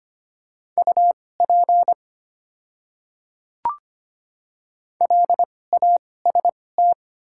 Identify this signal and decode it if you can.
Morse code, heard as a keyed tone, about 25 wpm, spaced as for 18 wpm, 700 Hz. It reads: UP LAST
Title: 2x Character Spacing - Top 500 Words - 25wpm